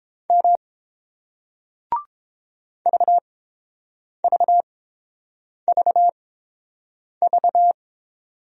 Morse code, heard as a keyed tone, about 30 wpm, spaced as for 7 wpm, 700 Hz. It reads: M VVVV